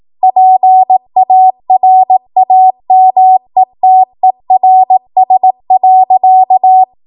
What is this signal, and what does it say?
Morse code, heard as a keyed tone, about 18 wpm, 750 Hz. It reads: PARAMETERS.
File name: Morse / 18WPM